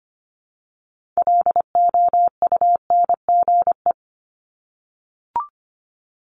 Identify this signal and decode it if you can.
Morse code, heard as a keyed tone, about 25 wpm, 700 Hz. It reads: LOUNGE
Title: Multi-letter Phonemes Repeated 3x 25wpm